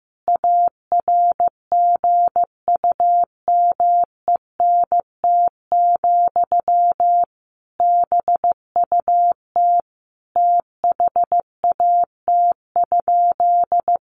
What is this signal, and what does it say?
Morse code, heard as a keyed tone, about 15 wpm, 700 Hz. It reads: ARGUMENT, BUT THAT?
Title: Sentences from Top 1400 Words 15wpm